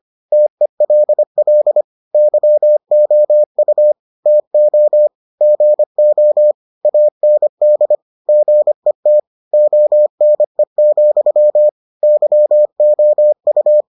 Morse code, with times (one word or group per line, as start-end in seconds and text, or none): 0.3-1.8 TELL
2.1-3.9 YOU
4.3-5.1 TO
5.4-6.5 GO
6.8-8.0 AND
8.3-9.2 GET
9.5-11.7 ONE,
12.0-13.8 YOU